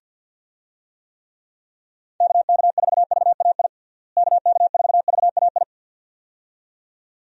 Morse code, heard as a keyed tone, about 50 wpm, 700 Hz. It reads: KK4VAI KK4VAI
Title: USA Call Signs - 6 Characters - Repeated 2x - 50wpm